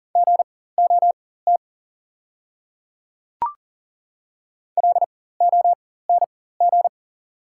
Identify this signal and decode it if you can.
Morse code, heard as a keyed tone, about 37 wpm, spaced as for 18 wpm, 700 Hz. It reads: GOT LONG
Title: 4x Character Spacing - Top 300 Words - 40wpm